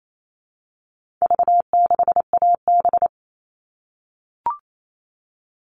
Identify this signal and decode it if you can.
Morse code, heard as a keyed tone, about 28 wpm, 700 Hz. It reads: V6AB
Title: All Cut Numbers - 4-Digit 28wpm